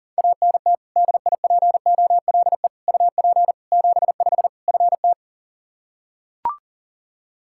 Morse code, heard as a keyed tone, about 40 wpm, 700 Hz. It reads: ANT DIPOLE UP 75 FT
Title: Practice QSO - Exchange 2 - Info - 40wpm